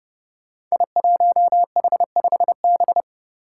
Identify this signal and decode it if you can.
Morse code, heard as a keyed tone, about 28 wpm, 700 Hz. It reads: I1H5B
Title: Five Challenging Letter-Number Combinations 30wpm